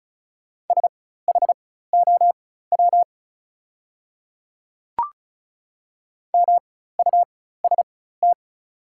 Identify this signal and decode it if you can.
Morse code, heard as a keyed tone, about 36 wpm, spaced as for 16 wpm, 690 Hz. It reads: SHOW MUST
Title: 4x Character Spacing - Top 300 Words - 35wpm